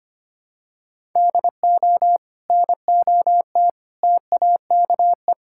Morse code, heard as a keyed tone, about 25 wpm, 700 Hz. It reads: DO NOT TAKE
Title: Sentences from Top 1000 Words - Review Part 5 - 25wpm